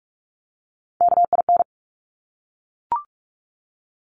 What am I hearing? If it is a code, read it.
Morse code, heard as a keyed tone, about 46 wpm, 700 Hz. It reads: KIN
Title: Top Three Letter Combinations in Words 45wpm